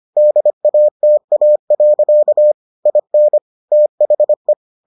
Morse code, heard as a keyed tone, about 25 wpm, 600 Hz. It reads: DATA. IN THE